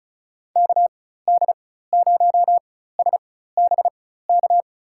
Morse code, heard as a keyed tone, about 34 wpm, spaced as for 16 wpm, 700 Hz. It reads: KD0SBK
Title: USA Call Signs - 6 Characters - 4x Character Spacing - 35wpm